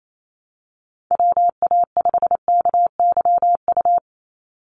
Morse code, heard as a keyed tone, about 28 wpm, 700 Hz. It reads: WA5KYU
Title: USA Call Signs 28wpm